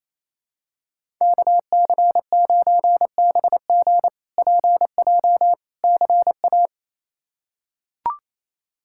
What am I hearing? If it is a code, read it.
Morse code, heard as a keyed tone, about 28 wpm, 700 Hz. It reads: KC9BG PJ CA
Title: K1USN SST QSO Practice - 28wpm